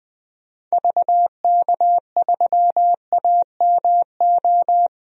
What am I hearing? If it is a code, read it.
Morse code, heard as a keyed tone, about 20 wpm, 700 Hz. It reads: VK3AMO
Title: DXCC-100 Call Signs - 6 Characters - Repeated 3x - 20wpm